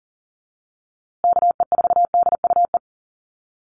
Morse code, heard as a keyed tone, about 40 wpm, 700 Hz. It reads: KE4DUE
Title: USA Call Signs Repeated 2x 40wpm